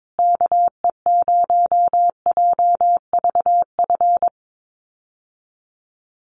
Morse code, heard as a keyed tone, about 22 wpm, 700 Hz. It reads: KE0JVF
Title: USA Call Signs Repeated 2x 22wpm